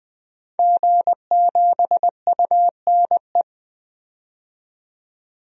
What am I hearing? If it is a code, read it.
Morse code, heard as a keyed tone, about 21 wpm, 700 Hz. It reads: G7UNE
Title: DXCC-100 Call Signs - 5 Characters - 20wpm